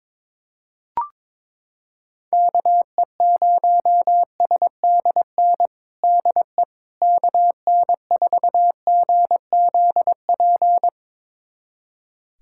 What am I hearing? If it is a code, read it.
Morse code, heard as a keyed tone, about 22 wpm, 700 Hz. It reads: KE0SDN DE KN4GZP
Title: USA Callsign DE Callsign - 6 Characters - 22wpm